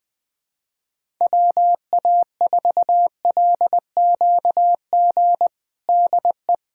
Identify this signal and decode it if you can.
Morse code, heard as a keyed tone, about 20 wpm, 700 Hz. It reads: WA4LQG DE
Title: USA Callsign DE Callsign - 6 Characters - 20wpm